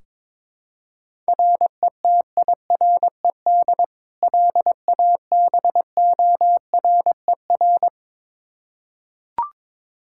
Morse code, heard as a keyed tone, about 22 wpm, 700 Hz. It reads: RETIRED LABORER
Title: Practice QSO - Exchange 3 - More Info - 22wpm